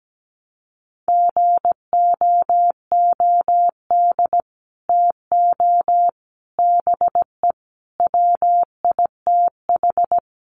Morse code, heard as a keyed tone, about 17 wpm, 700 Hz. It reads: GOOD TO BE WITH
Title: Sets of 5 Words Encore2 17wpm